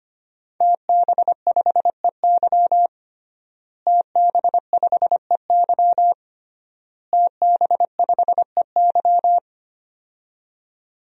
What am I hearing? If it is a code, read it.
Morse code, heard as a keyed tone, about 25 wpm, 700 Hz. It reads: TB5EY TB5EY TB5EY
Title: DXCC-100 Call Signs - 5 Characters - Repeated 3x - 25wpm